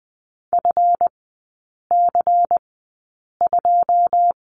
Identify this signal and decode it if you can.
Morse code, heard as a keyed tone, about 20 wpm, spaced as for 8 wpm, 700 Hz. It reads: FC2
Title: 138 Lesson 23e - Letter f - Three Characters 20wpm